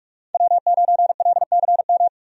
Morse code, heard as a keyed tone, about 45 wpm, 700 Hz. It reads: W9LCM
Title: USA Call Signs - 5 Characters - 45wpm